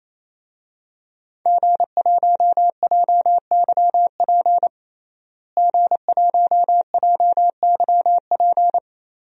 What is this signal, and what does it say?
Morse code, heard as a keyed tone, about 27 wpm, 700 Hz. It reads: G1JYP G1JYP
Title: DXCC-100 Call Signs - 5 Characters - Repeated 3x - 28wpm